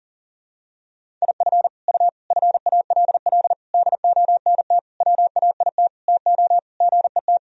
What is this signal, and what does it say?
Morse code, heard as a keyed tone, about 40 wpm, 700 Hz. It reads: IF U FALL DONT WAIT TO GET
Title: Sentences from Top 500 Words and Abbreviations 40wpm